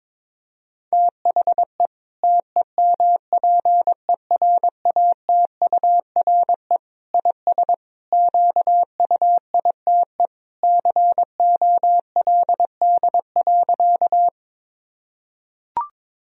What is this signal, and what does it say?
Morse code, heard as a keyed tone, about 22 wpm, 700 Hz. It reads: THE TEMPERATURE IS QUITE COLD.
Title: Sentences from Top 1000 Words - Review Part 4 - 22wpm